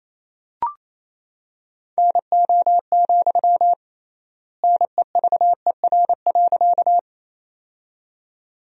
Morse code, heard as a keyed tone, about 27 wpm, 700 Hz. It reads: NO, NEVER.
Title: 3x Word-Spacing - Sentences from Top 300 Words - 28wpm